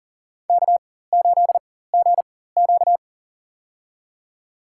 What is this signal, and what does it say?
Morse code, heard as a keyed tone, about 40 wpm, spaced as for 19 wpm, 700 Hz. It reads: K8GQ
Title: USA Call Signs - 4 Characters - 4x Character Spacing - 40wpm